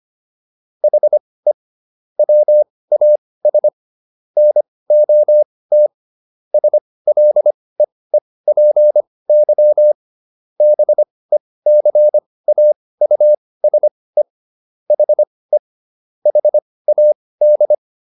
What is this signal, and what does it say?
Morse code, heard as a keyed tone, about 25 wpm, spaced as for 18 wpm, 600 Hz. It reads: HE WAS NOT SLEEPY BECAUSE HE HAD